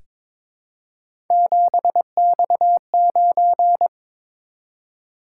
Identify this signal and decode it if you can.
Morse code, heard as a keyed tone, about 22 wpm, 700 Hz. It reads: 7X9
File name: practice / 22wpm / Three Letter-Number Combinations - 22wpm